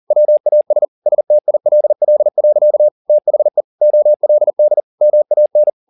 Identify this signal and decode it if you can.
Morse code, heard as a keyed tone, about 40 wpm, 600 Hz. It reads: WAS STILL. THE OLD MAN
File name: the old man and the sea 2 40wpm